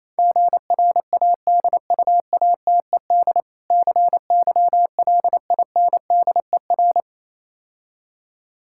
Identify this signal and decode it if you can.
Morse code, heard as a keyed tone, about 27 wpm, 700 Hz. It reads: GRADUATED CYLINDER
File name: Easy Words - Speed-Racing with Hints - 22wpm